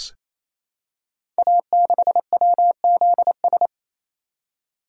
Morse code, heard as a keyed tone, about 29 wpm, 700 Hz. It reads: A6WZS
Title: Code Groups - 5 Letter:Numbers - 28wpm